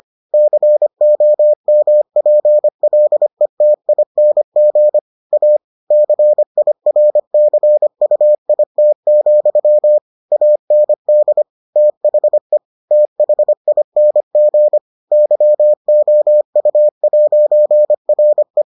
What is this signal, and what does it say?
Morse code, heard as a keyed tone, about 25 wpm, 600 Hz. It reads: COMPLETING A CIRCUIT, AND THE THING YOU'RE